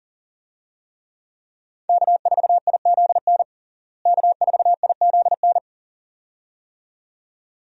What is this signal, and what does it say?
Morse code, heard as a keyed tone, about 40 wpm, 700 Hz. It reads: K4IZN K4IZN
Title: USA Call Signs - 5 Characters - Repeated 2x - 40wpm